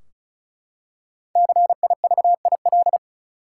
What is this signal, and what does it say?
Morse code, heard as a keyed tone, about 36 wpm, 700 Hz. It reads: CIVIL
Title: Five Letter Words - 35wpm